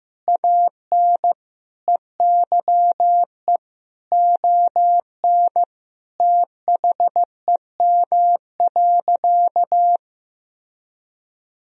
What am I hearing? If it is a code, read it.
Morse code, heard as a keyed tone, about 16 wpm, 700 Hz. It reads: AN EYE ON THEM.
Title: Sentences from Top 1000 Words - Review Part 1 - 15wpm